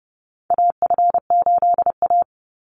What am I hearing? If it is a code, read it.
Morse code, heard as a keyed tone, about 30 wpm, 700 Hz. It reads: AF8A
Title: Call Signs - 30wpm